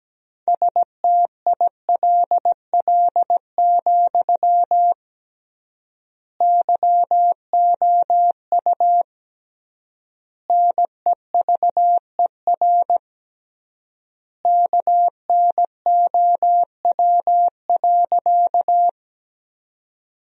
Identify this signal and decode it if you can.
Morse code, heard as a keyed tone, about 17 wpm, 700 Hz. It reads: STILL, YOU NEVER KNOW.
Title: 3x Word-Spacing - Sentences from Top 300 Words - 17wpm